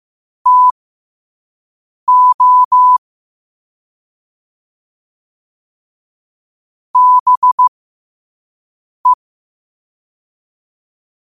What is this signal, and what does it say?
Morse code, heard as a keyed tone, about 15 wpm, spaced as for 5 wpm, 1000 Hz. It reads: TO BE
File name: qotdfiles / 5wpm